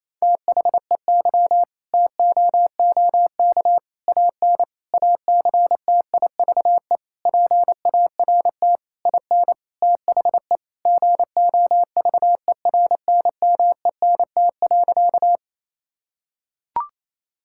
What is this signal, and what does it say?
Morse code, heard as a keyed tone, about 28 wpm, 700 Hz. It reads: THEY TOOK AN ACTIVE PART IN THE GOVERNMENT.
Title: Sentences from Top 1000 Words - Review Part 6 - 28wpm